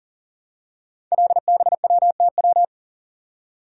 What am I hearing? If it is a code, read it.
Morse code, heard as a keyed tone, about 39 wpm, 700 Hz. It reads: LBWTW
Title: Code Groups - 5 Letter:Numbers - 40wpm